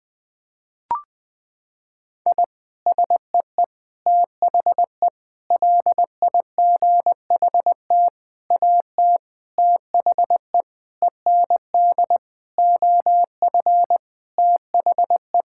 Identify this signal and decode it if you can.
Morse code, heard as a keyed tone, about 19 wpm, 700 Hz. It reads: I SEE THE LIGHT AT THE END OF THE
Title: Sentences with 8-Word Phrases - 20wpm